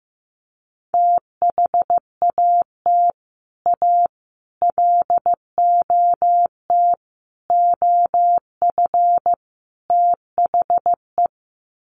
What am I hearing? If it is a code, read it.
Morse code, heard as a keyed tone, about 15 wpm, 700 Hz. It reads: THAT A LOT OF THE
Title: Sets of 5 Words Encore2 15wpm